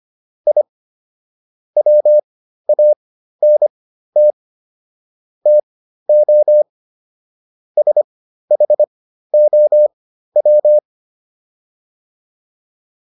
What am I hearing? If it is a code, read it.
Morse code, heard as a keyed tone, about 25 wpm, spaced as for 13 wpm, 600 Hz. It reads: I WANT TO SHOW